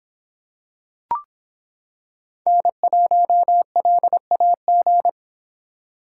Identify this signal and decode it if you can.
Morse code, heard as a keyed tone, about 25 wpm, 700 Hz. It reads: N1LAG